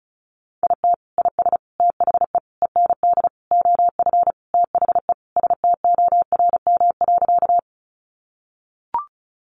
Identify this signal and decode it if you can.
Morse code, heard as a keyed tone, about 34 wpm, 700 Hz. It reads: IT IS THE END OF THE STORM.
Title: Sentences from Top 1500 Words 35wpm